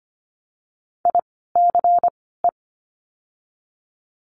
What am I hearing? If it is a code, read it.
Morse code, heard as a keyed tone, about 24 wpm, 700 Hz. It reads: I C E
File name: course / 25:10wpm / 094 Lesson 16f - Letter c - Words 25wpm